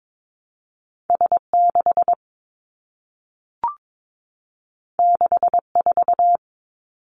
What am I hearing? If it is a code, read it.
Morse code, heard as a keyed tone, about 22 wpm, 700 Hz. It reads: S 6 6 4
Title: Two Challenging Letter-Number Combinations 22wpm